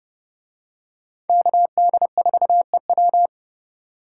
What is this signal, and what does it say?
Morse code, heard as a keyed tone, about 30 wpm, 700 Hz. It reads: KD4EW